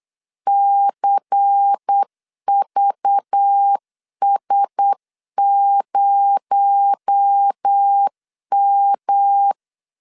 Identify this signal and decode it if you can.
Morse code, heard as a keyed tone, about 8 wpm, 800 Hz. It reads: CVS0M
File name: LEZIONI / 9 28 8wpm 3512